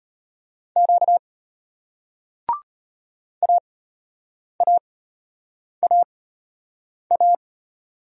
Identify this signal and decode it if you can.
Morse code, heard as a keyed tone, about 32 wpm, spaced as for 8 wpm, 700 Hz. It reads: Q AAAA